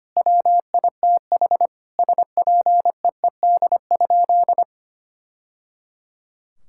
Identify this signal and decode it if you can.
Morse code, heard as a keyed tone, about 25 wpm, 700 Hz. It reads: WITH SPEED?
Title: Sentences from Top 1000 Words - Review Part 5 - 25wpm